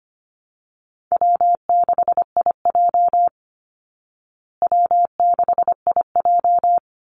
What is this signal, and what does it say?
Morse code, heard as a keyed tone, about 25 wpm, 700 Hz. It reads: W6IJ W6IJ
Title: Call Signs Repeated 2x - 25wpm